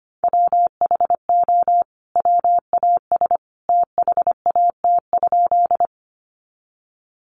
Morse code, heard as a keyed tone, about 25 wpm, 700 Hz. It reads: WHO WAS THAT?
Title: Sentences from Top 100 Words 25wpm